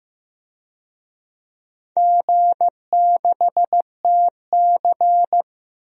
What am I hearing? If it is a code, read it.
Morse code, heard as a keyed tone, about 15 wpm, 700 Hz. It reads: G6TC
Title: DXCC-100 Call Signs - 4 Characters - 15wpm